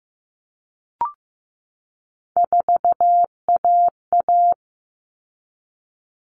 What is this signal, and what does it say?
Morse code, heard as a keyed tone, about 15 wpm, 700 Hz. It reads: 4AA
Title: All Cut Numbers - 3-Digit 15wpm